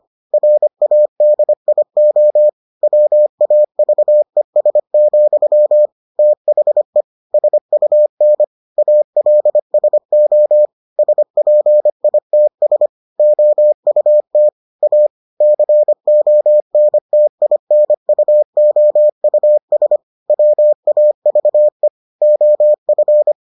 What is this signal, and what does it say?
Morse code, heard as a keyed tone, about 25 wpm, 600 Hz. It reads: RADIO WAVES, THE SUN ALSO SPITS OUT A CONTINUOUS WAVE OF